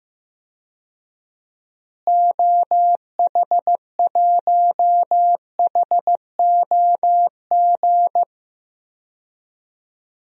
Morse code, heard as a keyed tone, about 16 wpm, 700 Hz. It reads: OH1HOG
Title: DXCC-100 Call Signs - 6 Characters - Repeated 2x - 15wpm